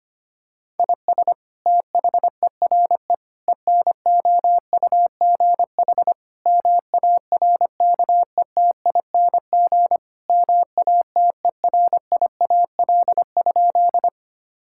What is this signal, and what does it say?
Morse code, heard as a keyed tone, about 25 wpm, 700 Hz. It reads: IS THERE ENOUGH MARKETING MATERIAL?
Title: Sentences from Top 1000 Words - Review Part 5 - 25wpm